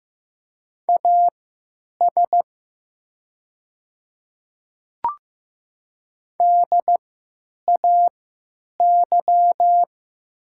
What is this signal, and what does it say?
Morse code, heard as a keyed tone, about 15 wpm, spaced as for 9 wpm, 700 Hz. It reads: AS DAY
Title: 3x Character Spacing - Top 200 Words - 15wpm